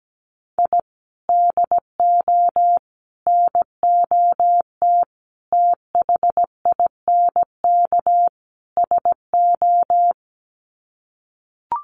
Morse code, heard as a keyed tone, about 17 wpm, 700 Hz. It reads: I DO NOT THINK SO
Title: Sets of 5 Words 17wpm